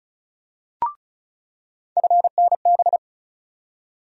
Morse code, heard as a keyed tone, about 36 wpm, 700 Hz. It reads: FNB
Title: USA Call Sign Postfixes - 3 Letters 35wpm